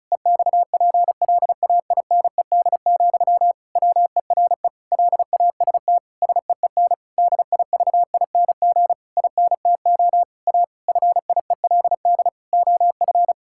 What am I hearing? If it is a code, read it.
Morse code, heard as a keyed tone, about 34 wpm, 700 Hz. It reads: EXPLAINED, WERE LAST SEEN DIVING INTO A FIELD OF
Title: Pangram Challenge v2 - 35wpm